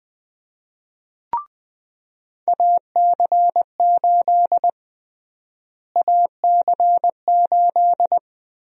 Morse code, heard as a keyed tone, about 20 wpm, 700 Hz. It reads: AC8 AC8
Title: USA Call Sign Prefixes - Repeated 2x - 20wpm